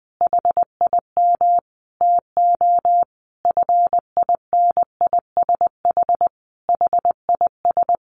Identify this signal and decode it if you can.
Morse code, heard as a keyed tone, about 20 wpm, 700 Hz. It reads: HIM TO FINISH HIS
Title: Sentences from Top 1200 Words 20wpm